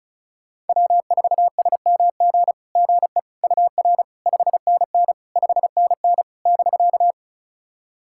Morse code, heard as a keyed tone, about 34 wpm, 690 Hz. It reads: W4SMG GE UR 5NN 5NN <BK>
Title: SOTA QSO Practice 35wpm